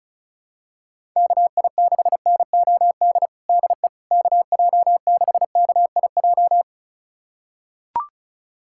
Morse code, heard as a keyed tone, about 36 wpm, 700 Hz. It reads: KI6NOD DE KJ6KIJ
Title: USA Callsign DE Callsign - 6 Characters - 35wpm